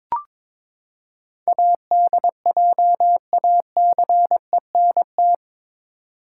Morse code, heard as a keyed tone, about 22 wpm, 700 Hz. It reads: ADJACENT
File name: practice / 22wpm / Eight Letter Words - 22wpm